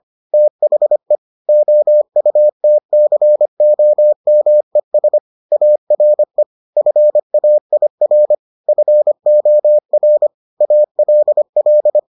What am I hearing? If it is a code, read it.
Morse code, heard as a keyed tone, about 25 wpm, 600 Hz. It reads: THE OUTCOMES ARE FAIR FOR ALL